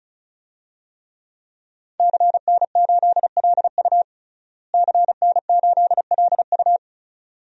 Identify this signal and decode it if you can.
Morse code, heard as a keyed tone, about 36 wpm, 700 Hz. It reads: CN8LU CN8LU
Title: DXCC-100 Call Signs - 5 Characters - Repeated 2x - 35wpm